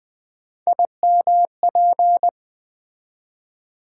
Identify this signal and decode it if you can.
Morse code, heard as a keyed tone, about 20 wpm, 700 Hz. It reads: IMP
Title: Build A Word - 3 Letters At A Time - 8-Letter Words - 20wpm